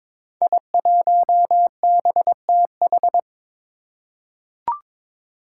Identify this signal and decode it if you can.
Morse code, heard as a keyed tone, about 22 wpm, 700 Hz. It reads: I1BTH
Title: Five Challenging Letter-Number Combinations 22wpm